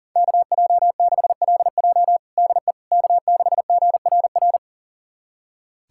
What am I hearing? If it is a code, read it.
Morse code, heard as a keyed tone, about 40 wpm, 700 Hz. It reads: KJ6LJ DE K6GRR
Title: USA Callsign DE Callsign - 5 Characters - 40wpm